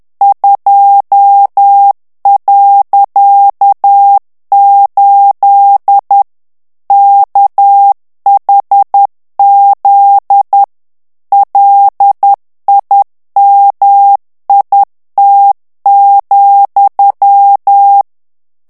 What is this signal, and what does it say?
Morse code, heard as a keyed tone, about 10 wpm, 800 Hz. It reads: 2.8 KHZ LIMIT,
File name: ANGLAIS texte clair 3 vitesse 10 mots minute